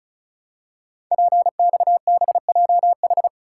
Morse code, heard as a keyed tone, about 36 wpm, 690 Hz. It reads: PXBJH
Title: Code Groups - 5 Letters - 35wpm